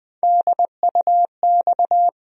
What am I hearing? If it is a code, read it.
Morse code, heard as a keyed tone, about 20 wpm, 700 Hz. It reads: DUX
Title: USA Call Sign Postfixes - 3 Letters 20wpm